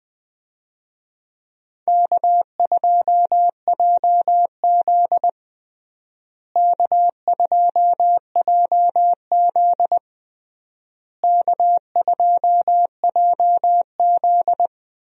Morse code, heard as a keyed tone, about 20 wpm, 700 Hz. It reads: K2JZ K2JZ K2JZ
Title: USA Call Signs - 4 Characters - Repeated 3x - 20wpm